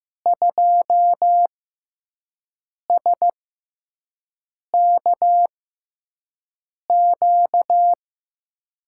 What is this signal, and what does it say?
Morse code, heard as a keyed tone, about 15 wpm, spaced as for 5 wpm, 700 Hz. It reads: 2SKQ
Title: USA Call Signs - 6 Characters - 6x Character Spacing - 15wpm